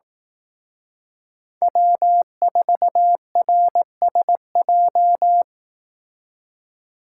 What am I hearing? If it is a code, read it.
Morse code, heard as a keyed tone, about 18 wpm, 700 Hz. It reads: W4RSJ
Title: Speed Racing - USA 1x3 Call Signs 15wpm